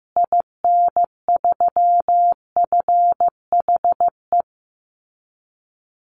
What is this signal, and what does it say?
Morse code, heard as a keyed tone, about 15 wpm, 700 Hz. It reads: IN3FHE
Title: Call Signs Repeated 2x - 15wpm